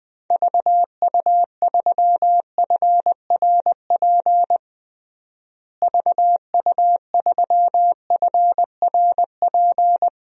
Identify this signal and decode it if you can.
Morse code, heard as a keyed tone, about 20 wpm, 700 Hz. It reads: VU3FRP VU3FRP
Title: DXCC-100 Call Signs - 6 Characters - Repeated 3x - 20wpm